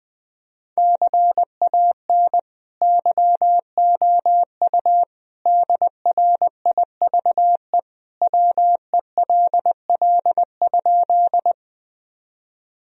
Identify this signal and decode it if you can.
Morse code, heard as a keyed tone, about 20 wpm, 700 Hz. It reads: CAN YOU DRIVE WELL?
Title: Sentences from Top 1000 Words - Review Part 2 - 20wpm